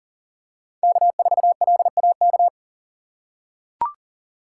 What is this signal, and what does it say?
Morse code, heard as a keyed tone, about 40 wpm, 700 Hz. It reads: K4LAK
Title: USA Call Signs - 5 Characters - 3x Character Spacing - 40wpm